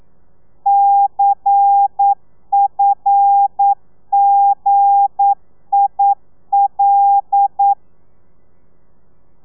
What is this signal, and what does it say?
Morse code, heard as a keyed tone, about 9 wpm, 800 Hz. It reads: CFGIL
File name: telegrafia / i2tfj / Lez11 40